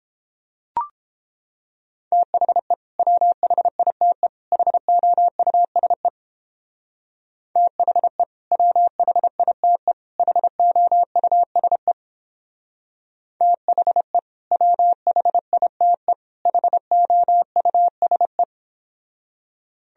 Morse code, heard as a keyed tone, about 31 wpm, 700 Hz. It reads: THE WHITE HOUSE THE WHITE HOUSE THE WHITE HOUSE